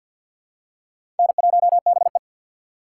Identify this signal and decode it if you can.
Morse code, heard as a keyed tone, about 54 wpm, 690 Hz. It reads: N1BE
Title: USA Call Signs - 4 Characters - 6x Character Spacing - 50wpm